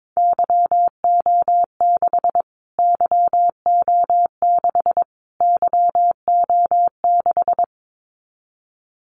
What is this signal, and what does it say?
Morse code, heard as a keyed tone, about 22 wpm, 700 Hz. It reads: YO6 YO6 YO6
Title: Callsign Prefixes for Top 25 DXCC - Repeated 3x - 22wpm